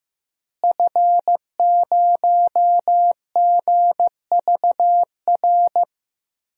Morse code, heard as a keyed tone, about 15 wpm, 700 Hz. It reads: F0GVR
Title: DXCC-100 Call Signs - 5 Characters - Repeated 3x - 15wpm